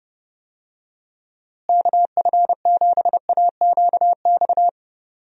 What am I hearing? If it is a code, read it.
Morse code, heard as a keyed tone, about 30 wpm, 700 Hz. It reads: KF7AQX
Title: USA Call Signs - 6 Characters - Repeated 3x - 30wpm